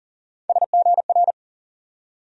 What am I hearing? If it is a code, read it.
Morse code, heard as a keyed tone, about 40 wpm, 700 Hz. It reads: SGR
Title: USA Call Sign Postfixes - 3 Letters 40wpm